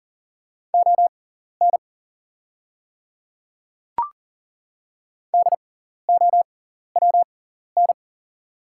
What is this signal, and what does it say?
Morse code, heard as a keyed tone, about 40 wpm, spaced as for 14 wpm, 690 Hz. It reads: ON DOWN
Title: 6x Character Spacing - Top 200 Words - 40wpm